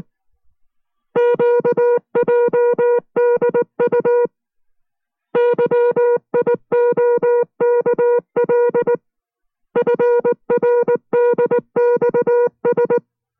Morse code, heard as a keyed tone, about 19 wpm, 480 Hz. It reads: QJDU YIOKL FRDXS